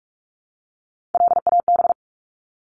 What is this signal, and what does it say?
Morse code, heard as a keyed tone, about 44 wpm, 700 Hz. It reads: LAB
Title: Top 1751-2000 Words 45wpm